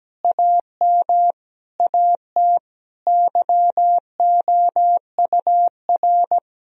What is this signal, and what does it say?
Morse code, heard as a keyed tone, about 17 wpm, 700 Hz. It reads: AM AT YOUR
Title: Sentences from Top 1000 Words - Review Part 4 - 17wpm